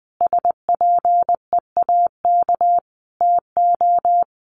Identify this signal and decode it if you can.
Morse code, heard as a keyed tone, about 20 wpm, 700 Hz. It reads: SPEAK TO
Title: Sentences from Top 1400 Words 20wpm